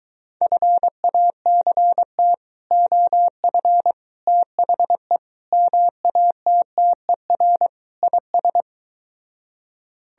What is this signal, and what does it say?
Morse code, heard as a keyed tone, about 22 wpm, 700 Hz. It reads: FACT OF THE MATTER IS